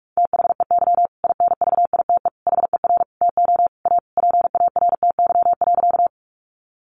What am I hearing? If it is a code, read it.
Morse code, heard as a keyed tone, about 46 wpm, 700 Hz. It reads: THEY INVITE HER TO A PARTY.
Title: Easy Phrases 45wpm